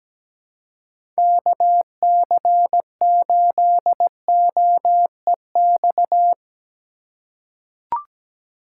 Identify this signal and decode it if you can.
Morse code, heard as a keyed tone, about 17 wpm, 700 Hz. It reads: KC8OEX
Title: USA Call Signs - 2x Character Spacing - 17wpm